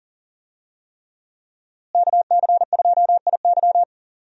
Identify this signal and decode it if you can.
Morse code, heard as a keyed tone, about 40 wpm, 700 Hz. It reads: KC2IY
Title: USA Call Signs - 5 Characters - Repeated 3x - 40wpm